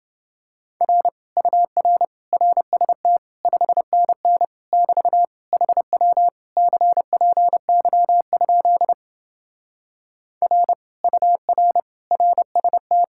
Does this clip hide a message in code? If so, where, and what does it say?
Yes: Morse code from 0.8 to 13.0 s, heard as a keyed tone, about 28 wpm, 700 Hz. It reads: R UR RST 5NN <BT> HW CPY? R UR RST